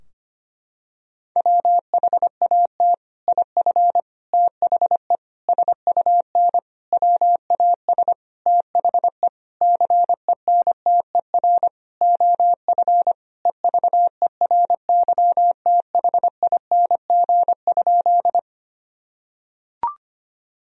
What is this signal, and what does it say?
Morse code, heard as a keyed tone, about 25 wpm, 700 Hz. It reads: WHAT IF THE SUN WAS THE CENTER OF EVERYTHING?
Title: Sentences from Top 1000 Words - Review Part 2 - 25wpm